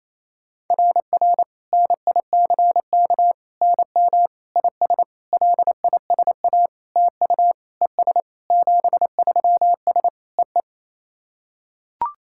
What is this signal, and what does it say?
Morse code, heard as a keyed tone, about 28 wpm, 690 Hz. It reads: RR NICK NM IS LISA TU ES 73S E E
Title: SOTA QSO Practice 28wpm